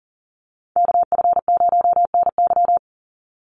Morse code, heard as a keyed tone, about 38 wpm, 700 Hz. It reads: KF0NY
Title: USA Call Signs 40wpm